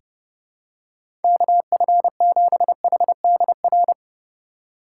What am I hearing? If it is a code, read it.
Morse code, heard as a keyed tone, about 30 wpm, 700 Hz. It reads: KF7HDR
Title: USA Call Signs - 6 Characters - 2x Character Spacing - 30wpm